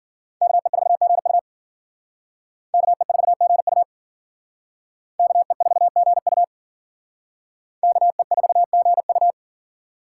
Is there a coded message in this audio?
Yes: Morse code from 0.4 to 9.3 s, heard as a keyed tone, about 51 wpm, 700 Hz. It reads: KE4GU KE4GU KE4GU KE4GU